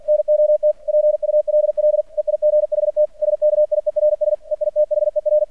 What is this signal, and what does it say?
Morse code, heard as a keyed tone, about 50 wpm, 600 Hz. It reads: NOT PULL EIGHT SKIERS EITHER
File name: archived / CRSnet